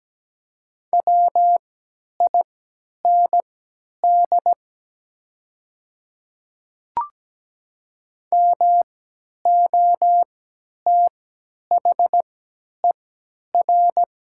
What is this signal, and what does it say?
Morse code, heard as a keyed tone, about 17 wpm, spaced as for 10 wpm, 700 Hz. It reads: WIND MOTHER
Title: 3x Character Spacing - Top 500 Words - 17wpm